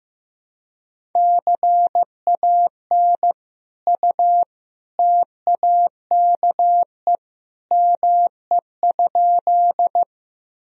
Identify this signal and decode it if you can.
Morse code, heard as a keyed tone, about 15 wpm, 700 Hz. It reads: CAN U TAKE ME?
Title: Sentences from Top 200 Words and Abbreviations 15wpm